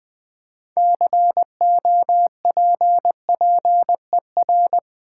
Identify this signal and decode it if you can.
Morse code, heard as a keyed tone, about 20 wpm, 690 Hz. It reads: COPPER
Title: Build A Word - 2 Letters At A Time - 6-Letter Words - 20wpm